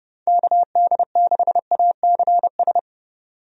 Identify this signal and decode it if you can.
Morse code, heard as a keyed tone, about 30 wpm, 700 Hz. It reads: KD6ACS
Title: USA Call Signs - 6 Characters - 30wpm